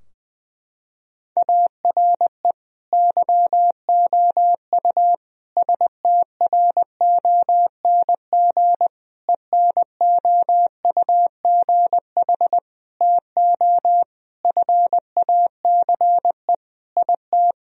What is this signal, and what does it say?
Morse code, heard as a keyed tone, about 20 wpm, 700 Hz. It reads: ARE YOU STRONG ENOUGH TO FACE IT
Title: Sentences from Top 1000 Words - Review Part 3 - 20wpm